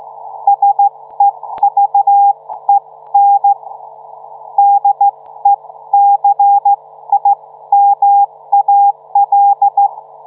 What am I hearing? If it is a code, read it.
Morse code, heard as a keyed tone, about 15 wpm, spaced as for 13 wpm, 800 Hz. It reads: SEVEN DECIMAL